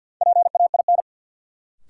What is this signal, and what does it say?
Morse code, heard as a keyed tone, about 53 wpm, 700 Hz. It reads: PAIN
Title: Sentences from Top 1600 Words - New Word Drill Alt - 50wpm